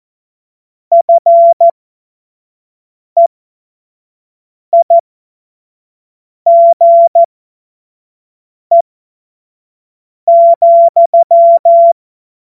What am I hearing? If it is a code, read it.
Morse code, heard as a keyed tone, about 14 wpm, spaced as for 5 wpm, 680 Hz. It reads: FEIGE,